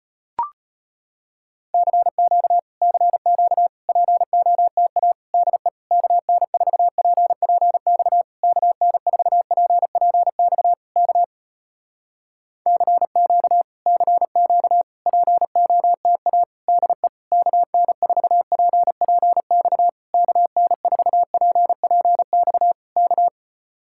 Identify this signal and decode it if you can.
Morse code, heard as a keyed tone, about 35 wpm, 700 Hz. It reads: CQ CQ POTA DE KN4PPX KN4PPX K CQ CQ POTA DE KN4PPX KN4PPX K